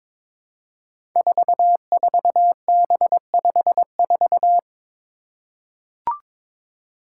Morse code, heard as a keyed tone, about 22 wpm, 700 Hz. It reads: 44B54
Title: Five Challenging Letter-Number Combinations 22wpm